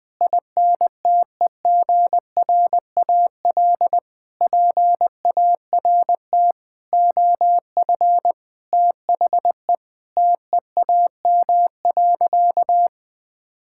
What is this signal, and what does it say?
Morse code, heard as a keyed tone, about 20 wpm, 700 Hz. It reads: INTEGRAL PART OF THE TEAM.
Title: Sentences with 5-Word Phrases - 20wpm